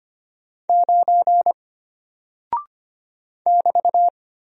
Morse code, heard as a keyed tone, about 25 wpm, 700 Hz. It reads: 9 <BT>
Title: Single Character (Letter:Number:Punctuation:Prosign) Alt 25wpm